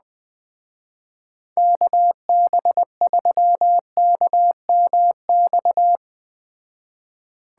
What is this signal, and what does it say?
Morse code, heard as a keyed tone, about 20 wpm, 700 Hz. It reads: KB3KMX
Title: Speed Racing - USA 2x3 Call Signs 17wpm